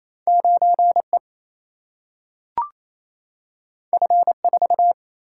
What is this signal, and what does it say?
Morse code, heard as a keyed tone, about 28 wpm, 700 Hz. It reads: 9E F4
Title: Single-Letter Single-Number 28wpm